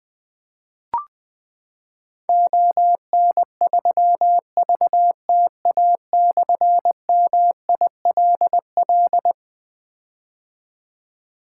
Morse code, heard as a keyed tone, about 21 wpm, 700 Hz. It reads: ON3VTA/MILL
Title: Difficult Call Signs 20wpm